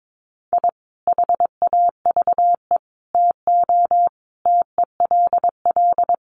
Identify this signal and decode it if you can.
Morse code, heard as a keyed tone, about 22 wpm, 700 Hz. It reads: I HAVE TO TELL
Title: Sets of 5 Words 22wpm